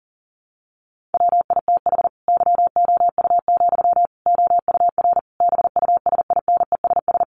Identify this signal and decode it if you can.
Morse code, heard as a keyed tone, about 40 wpm, 700 Hz. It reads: WITH YOU, OUR BUSINESS
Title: Sentences from Top 1400 Words 40wpm